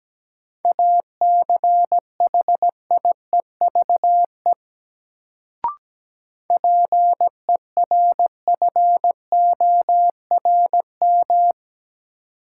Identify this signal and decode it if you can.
Morse code, heard as a keyed tone, about 17 wpm, 700 Hz. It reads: ACHIEVE PERFORM
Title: Sentences from Top 1000 Words - Review Part 5 - New Word Drill - 17wpm